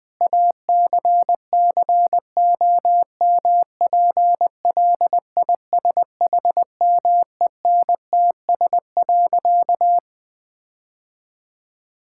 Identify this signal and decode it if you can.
Morse code, heard as a keyed tone, about 20 wpm, 700 Hz. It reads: ACCOMPLISHMENTS.
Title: Difficult Sentences with Hints - 20wpm